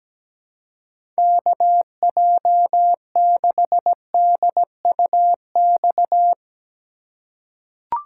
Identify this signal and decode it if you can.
Morse code, heard as a keyed tone, about 17 wpm, 700 Hz. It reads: KJ6DUX
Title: USA Call Signs - 6 Characters - 3x Character Spacing - 17wpm